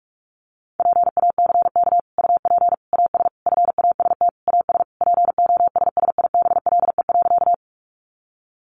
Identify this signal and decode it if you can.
Morse code, heard as a keyed tone, about 44 wpm, 700 Hz. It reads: PACK UP AS FAST AS POSSIBLE.
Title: Sentences from Top 1300 Words 45wpm